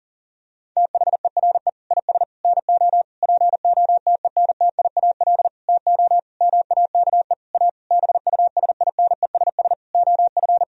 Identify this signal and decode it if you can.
Morse code, heard as a keyed tone, about 40 wpm, 700 Hz. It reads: THERE IS NO POTENTIAL TO MAKE A BUSINESS OF
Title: Sentences from Top 1000 Words - Review Part 4 - 40wpm